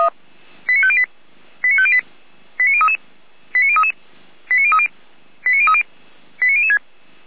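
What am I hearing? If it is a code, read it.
Five-tone selective call (ZVEI), heard as quick runs of five tones, tones of about 70 ms. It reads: 89508 89508 89033 89033 89030 89030 89006
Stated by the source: ZVEI five-tone call received by SDR.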